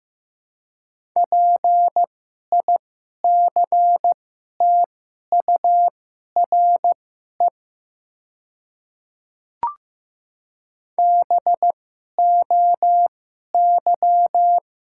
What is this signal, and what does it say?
Morse code, heard as a keyed tone, about 15 wpm, spaced as for 11 wpm, 700 Hz. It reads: PICTURE BOY
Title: 2x Character Spacing - Top 300 Words - 15wpm